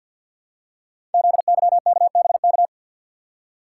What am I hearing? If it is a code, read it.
Morse code, heard as a keyed tone, about 48 wpm, 700 Hz. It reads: ZYKBK
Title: Code Groups - 5 Letters - 50wpm